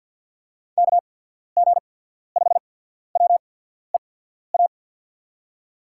Morse code, heard as a keyed tone, about 54 wpm, 700 Hz. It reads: K G 5 W E A
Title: USA Call Signs - 6 Characters - 8x Character Spacing - 50wpm